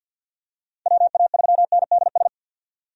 Morse code, heard as a keyed tone, about 51 wpm, 700 Hz. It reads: WA3NDS
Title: USA Call Signs - 6 Characters - 50wpm